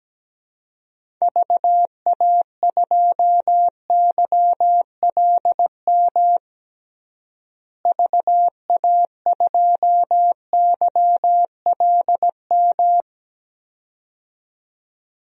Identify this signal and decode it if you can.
Morse code, heard as a keyed tone, about 17 wpm, 700 Hz. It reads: VA2YLM VA2YLM
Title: DXCC-100 Call Signs - 6 Characters - Repeated 2x - 17wpm